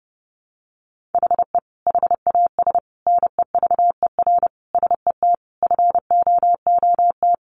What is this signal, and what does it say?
Morse code, heard as a keyed tone, about 30 wpm, 700 Hz. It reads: HE HAS NEVER SET FOOT
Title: Sentences from Top 1500 Words 30wpm